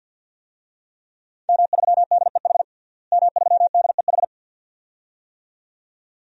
Morse code, heard as a keyed tone, about 51 wpm, 700 Hz. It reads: M3DEH M3DEH
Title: DXCC-100 Call Signs - 5 Characters - Repeated 2x - 50wpm